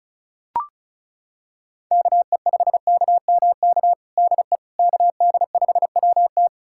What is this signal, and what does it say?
Morse code, heard as a keyed tone, about 36 wpm, 700 Hz. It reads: KE5KMK DE KD5WT
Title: USA Callsign DE Callsign 35wpm